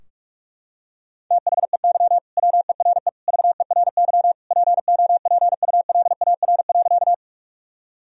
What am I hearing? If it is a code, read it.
Morse code, heard as a keyed tone, about 46 wpm, 700 Hz. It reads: THEY WERE VERY POPULAR.
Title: Sentences from Top 1000 Words - Review Part 3 - 45wpm